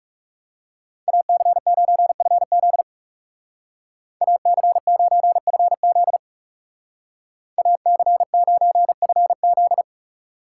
Morse code, heard as a keyed tone, about 40 wpm, 700 Hz. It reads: AC9FZ AC9FZ AC9FZ